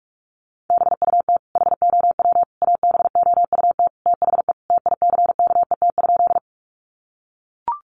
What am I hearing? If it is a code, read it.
Morse code, heard as a keyed tone, about 44 wpm, 700 Hz. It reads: BUT HOW ABOUT THE TICKET?
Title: Sentences from Top 1400 Words 45wpm